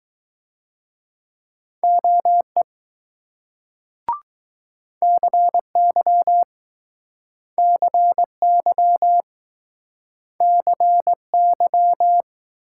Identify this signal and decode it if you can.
Morse code, heard as a keyed tone, about 20 wpm, 700 Hz. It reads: OE CY CY CY